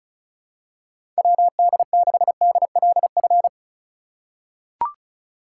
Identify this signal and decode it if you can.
Morse code, heard as a keyed tone, about 34 wpm, 700 Hz. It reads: WD6DLF
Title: USA Call Signs - 6 Characters - 3x Character Spacing - 35wpm